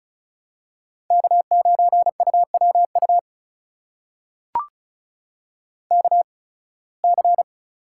Morse code, heard as a keyed tone, about 36 wpm, spaced as for 23 wpm, 700 Hz. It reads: K9UWU KC
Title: USA Call Signs - 5 Characters - 8x Character Spacing - 35wpm